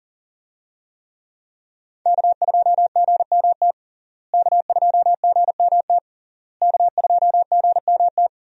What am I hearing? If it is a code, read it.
Morse code, heard as a keyed tone, about 40 wpm, 700 Hz. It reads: K2GMT K2GMT K2GMT
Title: USA Call Signs - 5 Characters - Repeated 3x - 40wpm